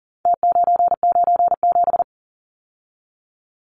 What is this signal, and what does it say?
Morse code, heard as a keyed tone, about 40 wpm, 700 Hz. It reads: T997
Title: Numbers - Contest Style - 4-Digit 40wpm